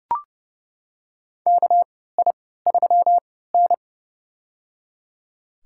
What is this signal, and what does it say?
Morse code, heard as a keyed tone, about 30 wpm, spaced as for 17 wpm, 700 Hz. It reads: KI3N
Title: USA Call Signs - 4 Characters - 3x Character Spacing - 30wpm